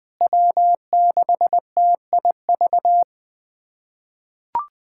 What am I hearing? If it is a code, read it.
Morse code, heard as a keyed tone, about 20 wpm, 700 Hz. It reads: W6TIV
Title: USA Callsign DE Callsign 20wpm